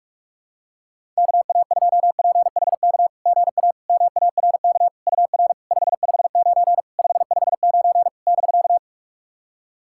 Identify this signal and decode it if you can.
Morse code, heard as a keyed tone, about 41 wpm, 690 Hz. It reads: KA2PHK GA MARK UR 559 559 <BK>
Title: SOTA QSO Practice 45wpm